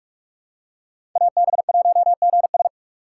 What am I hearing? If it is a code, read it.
Morse code, heard as a keyed tone, about 45 wpm, 700 Hz. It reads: AB1GS
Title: USA Call Signs - 5 Characters - 3x Character Spacing - 45wpm